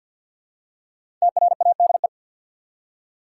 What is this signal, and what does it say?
Morse code, heard as a keyed tone, about 49 wpm, 700 Hz. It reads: TRADE
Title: Five Letter Words - 50wpm